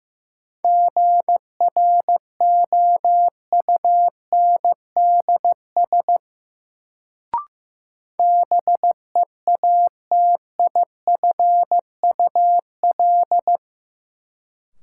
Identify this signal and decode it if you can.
Morse code, heard as a keyed tone, about 15 wpm, 700 Hz. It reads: GROUNDS BEATIFUL
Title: Sentences from Top 1300 Words - New Word Drill - 15wpm